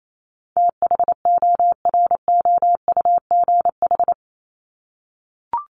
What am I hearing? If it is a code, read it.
Morse code, heard as a keyed tone, about 28 wpm, 700 Hz. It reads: THOROUGH
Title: LY Suffix - Top 100 Words with Base Words 28wpm